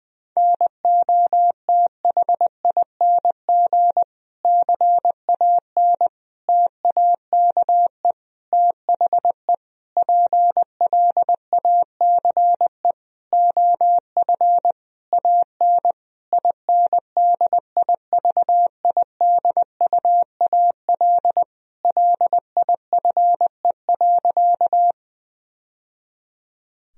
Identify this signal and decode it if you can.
Morse code, heard as a keyed tone, about 20 wpm, 700 Hz. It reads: NOTHING CAN TAKE THE PLACE OF AN INDIVIDUAL LIFE.
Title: Sentences from Top 1000 Words - Review Part 4 - 20wpm